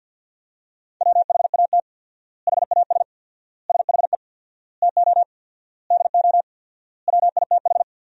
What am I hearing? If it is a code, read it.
Morse code, heard as a keyed tone, about 50 wpm, 700 Hz. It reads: WHAT HAS SHE TO DO WITH
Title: 4x Word-Spacing - Sentences from Top 100 Words - 50wpm